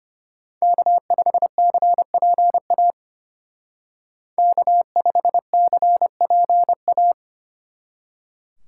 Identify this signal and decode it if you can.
Morse code, heard as a keyed tone, about 30 wpm, 700 Hz. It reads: K5CPA K5CPA
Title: Speed Racing - USA Call Signs 25wpm